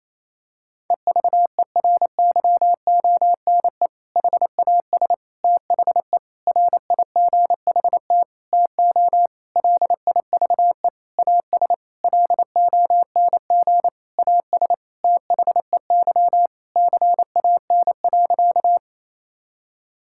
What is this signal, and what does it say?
Morse code, heard as a keyed tone, about 28 wpm, 700 Hz. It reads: EVERYONE HAS THE RIGHT TO LIVE AS LONG AS THEY CAN.
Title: Sentences from Top 1000 Words - Review Part 5 - 28wpm